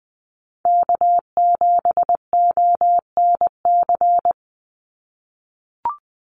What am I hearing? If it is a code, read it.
Morse code, heard as a keyed tone, about 20 wpm, 700 Hz. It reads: K7ONC
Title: USA Call Signs 20wpm